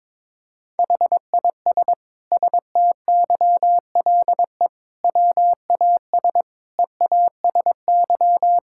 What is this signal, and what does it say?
Morse code, heard as a keyed tone, about 22 wpm, 700 Hz. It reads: HIS STYLE WAS EASY
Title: Sentences from Top 1000 Words - Review Part 5 - 22wpm